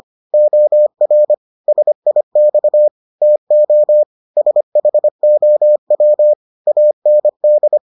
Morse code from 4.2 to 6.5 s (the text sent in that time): SHOW